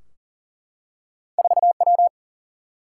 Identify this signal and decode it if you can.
Morse code, heard as a keyed tone, about 43 wpm, 700 Hz. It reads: 4 W
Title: Single-Letter Single-Number 40wpm